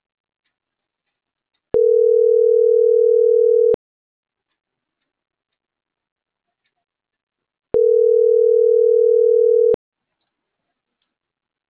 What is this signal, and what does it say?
Ringback tone (North America), 440 + 480 Hz.